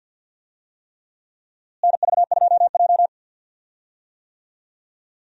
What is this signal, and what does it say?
Morse code, heard as a keyed tone, about 52 wpm, 700 Hz. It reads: NV2J
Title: USA Call Signs - 4 Characters - Repeated 3x - 50wpm